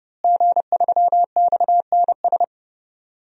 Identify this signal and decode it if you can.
Morse code, heard as a keyed tone, about 30 wpm, 700 Hz. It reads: G3XNS
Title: DXCC-100 Call Signs - 5 Characters - Repeated 3x - 30wpm